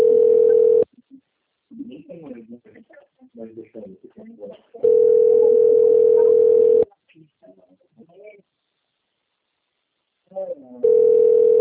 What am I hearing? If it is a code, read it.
Ringback tone (North America), 440 + 480 Hz, with speech.
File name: out- - - - - .54549